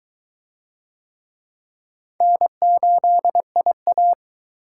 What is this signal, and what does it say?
Morse code, heard as a keyed tone, about 22 wpm, 700 Hz. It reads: N8IA